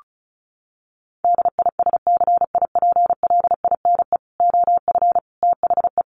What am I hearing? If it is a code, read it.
Morse code, heard as a keyed tone, about 34 wpm, 700 Hz. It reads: DISCIPLINE OF THE
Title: Sentences from Top 900 Words 35wpm